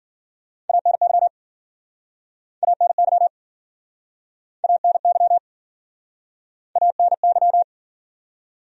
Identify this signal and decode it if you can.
Morse code, heard as a keyed tone, about 52 wpm, 700 Hz. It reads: ANY ANY ANY ANY